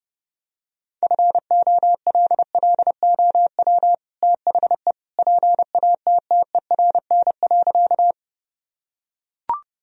Morse code, heard as a keyed tone, about 30 wpm, 700 Hz. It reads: FOLLOW THE PATTERN.
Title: Sentences from Top 1000 Words - Review Part 2 - 30wpm